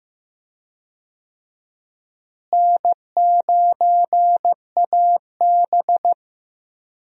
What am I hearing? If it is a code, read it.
Morse code, heard as a keyed tone, about 15 wpm, 700 Hz. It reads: N9AB
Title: USA Call Signs - 4 Characters - Repeated 2x - 15wpm